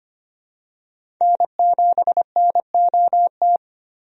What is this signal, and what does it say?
Morse code, heard as a keyed tone, about 25 wpm, 700 Hz. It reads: N7NOT
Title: USA Call Signs - 5 Characters - Repeated 3x - 25wpm